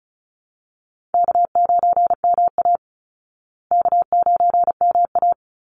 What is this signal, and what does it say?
Morse code, heard as a keyed tone, about 34 wpm, 700 Hz. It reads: K9MA K9MA
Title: Easy Call Signs Repeated 2x - 35wpm